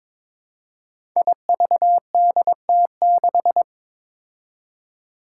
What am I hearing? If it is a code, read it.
Morse code, heard as a keyed tone, about 22 wpm, 700 Hz. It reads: IVDT6
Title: Five Challenging Letter-Number Combinations 22wpm